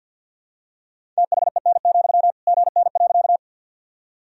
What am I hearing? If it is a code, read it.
Morse code, heard as a keyed tone, about 46 wpm, 700 Hz. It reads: THEN, GN.
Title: Sentences from Top 300 Words and Abbreviations 50wpm